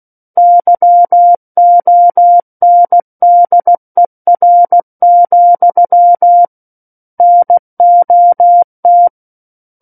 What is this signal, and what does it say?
Morse code, heard as a keyed tone, about 16 wpm, 700 Hz. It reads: YONDER, NOT
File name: Book One / 07-Chapter 7 - 16 WPM